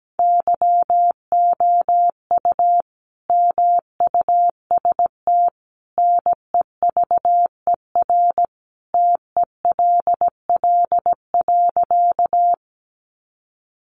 Morse code, heard as a keyed tone, about 17 wpm, 700 Hz. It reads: YOU MUST NEVER TELL.
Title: Sentences from Top 300 Words 17wpm